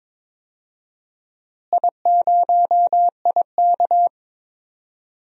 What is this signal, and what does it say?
Morse code, heard as a keyed tone, about 22 wpm, 700 Hz. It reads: I0IK
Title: DXCC-100 Call Signs - 4 Characters - Repeated 3x - 22wpm